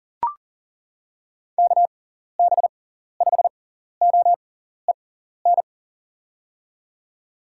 Morse code, heard as a keyed tone, about 37 wpm, spaced as for 14 wpm, 700 Hz. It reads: KB5OEN
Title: USA Call Signs - 6 Characters - 6x Character Spacing - 40wpm